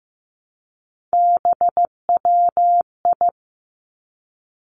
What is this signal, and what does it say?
Morse code, heard as a keyed tone, about 15 wpm, 700 Hz. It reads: BWI
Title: Three Challenging Letter-Number Combinations 15wpm